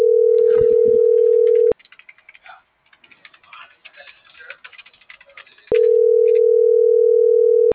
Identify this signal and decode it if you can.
Ringback tone (North America), 440 + 480 Hz.